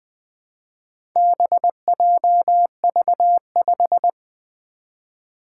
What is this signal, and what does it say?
Morse code, heard as a keyed tone, about 20 wpm, 700 Hz. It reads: BJV5
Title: Four Challenging Letter-Number Combinations 20wpm